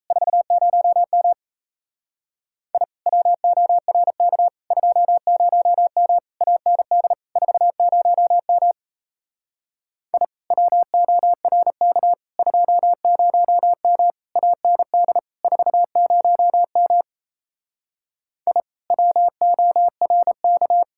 Morse code, heard as a keyed tone, about 37 wpm, 700 Hz. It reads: 40M I WORK 20M AND 40M I WORK 20M AND 40M I WORK